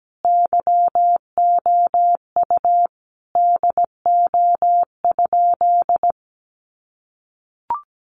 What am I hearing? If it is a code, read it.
Morse code, heard as a keyed tone, about 17 wpm, 700 Hz. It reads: YOU DO?
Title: Sentences from Top 100 Words 17wpm